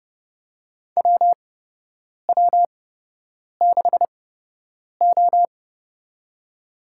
Morse code, heard as a keyed tone, about 30 wpm, spaced as for 8 wpm, 700 Hz. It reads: WW6O
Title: USA Call Signs - 4 Characters - 8x Character Spacing - 30wpm